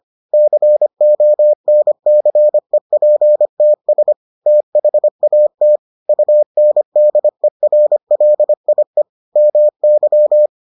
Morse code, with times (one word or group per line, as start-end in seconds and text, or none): 0.3-4.1 CONCEPTS
4.5-5.8 THAT
6.1-9.0 UNDERLIE
9.4-10.5 MY